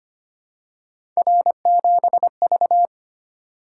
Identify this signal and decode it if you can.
Morse code, heard as a keyed tone, about 25 wpm, 700 Hz. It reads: R7V
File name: Three Letter-Number Combinations - 25wpm